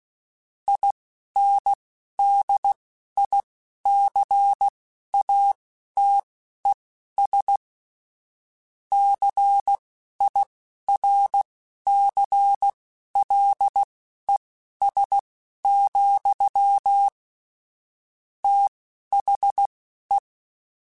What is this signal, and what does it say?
Morse code, heard as a keyed tone, about 16 wpm, spaced as for 11 wpm, 800 Hz. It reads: INDICATES CIRCLES, THE
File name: Technical Subjects / Technical Subjects-mp3s / XC @10